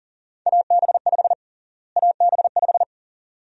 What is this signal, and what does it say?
Morse code, heard as a keyed tone, about 40 wpm, 700 Hz. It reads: AB5 AB5
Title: USA Call Sign Prefixes - Repeated 2x - 40wpm